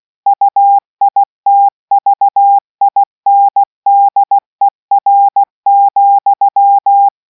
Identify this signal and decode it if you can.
Morse code, heard as a keyed tone, about 16 wpm, 800 Hz. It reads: UITVINDER,